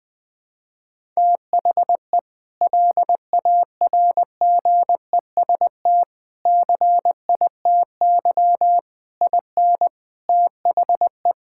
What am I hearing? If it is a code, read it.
Morse code, heard as a keyed tone, about 20 wpm, 700 Hz. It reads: THE LARGEST CITY IN THE
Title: Sentences with 5-Word Phrases - 20wpm